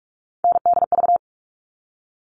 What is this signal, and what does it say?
Morse code, heard as a keyed tone, about 44 wpm, 700 Hz. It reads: NDV
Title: All Cut Numbers - 3-Digit 45wpm